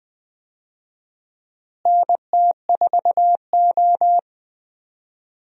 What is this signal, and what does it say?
Morse code, heard as a keyed tone, about 20 wpm, 700 Hz. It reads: NT4O
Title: USA Call Signs - 4 Characters - Repeated 2x - 20wpm